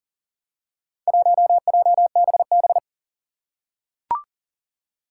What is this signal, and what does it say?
Morse code, heard as a keyed tone, about 40 wpm, 700 Hz. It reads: 1JBB
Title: Four Challenging Letter-Number Combinations 40wpm